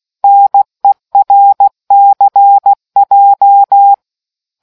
Morse code, heard as a keyed tone, about 16 wpm, 800 Hz. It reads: NERCJ